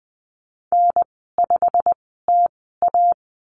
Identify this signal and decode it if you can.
Morse code, heard as a keyed tone, about 20 wpm, spaced as for 15 wpm, 700 Hz. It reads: N5TA
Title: 128 Lesson 21g - Number 6 - Call Signs 20wpm